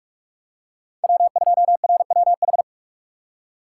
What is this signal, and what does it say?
Morse code, heard as a keyed tone, about 41 wpm, 700 Hz. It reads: W2RWH
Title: USA Call Signs - 6x Character Spacing - 45wpm